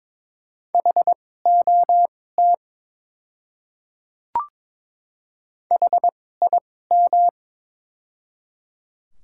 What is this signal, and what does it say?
Morse code, heard as a keyed tone, about 22 wpm, spaced as for 16 wpm, 700 Hz. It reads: HOT HIM
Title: 2x Character Spacing - Top 500 Words - 22wpm